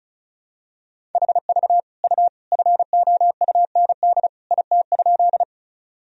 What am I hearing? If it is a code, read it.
Morse code, heard as a keyed tone, about 36 wpm, 700 Hz. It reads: HV U FOUND IT?
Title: Sentences from Top 300 Words and Abbreviations 35wpm